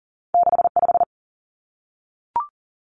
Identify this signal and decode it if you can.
Morse code, heard as a keyed tone, about 42 wpm, 700 Hz. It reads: 6 5
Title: Two Challenging Letter-Number Combinations 40wpm